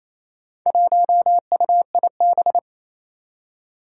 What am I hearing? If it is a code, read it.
Morse code, heard as a keyed tone, about 27 wpm, 700 Hz. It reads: 1UIB
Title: Four Challenging Letter-Number Combinations 28wpm